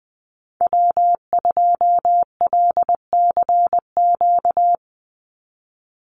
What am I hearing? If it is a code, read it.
Morse code, heard as a keyed tone, about 20 wpm, 700 Hz. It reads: W2LCQ
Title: Call Signs Repeated 3x - 20wpm